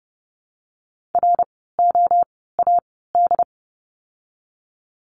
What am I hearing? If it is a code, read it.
Morse code, heard as a keyed tone, about 30 wpm, spaced as for 17 wpm, 700 Hz. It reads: ROAD
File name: Lesson 16f - Letter c - Words 30wpm